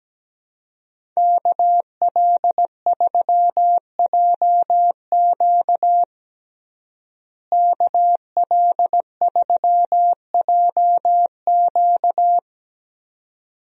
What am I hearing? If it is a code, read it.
Morse code, heard as a keyed tone, about 17 wpm, 700 Hz. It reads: KL3JQ KL3JQ
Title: USA Call Signs - 5 Characters - Repeated 2x - 17wpm